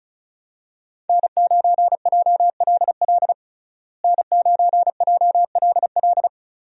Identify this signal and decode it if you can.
Morse code, heard as a keyed tone, about 34 wpm, 700 Hz. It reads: N9JLL N9JLL
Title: USA Call Signs - 5 Characters - Repeated 3x - 35wpm